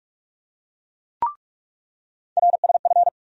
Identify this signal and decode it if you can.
Morse code, heard as a keyed tone, about 45 wpm, 700 Hz. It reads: RSF
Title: USA Call Sign Postfixes - 3 Letters 45wpm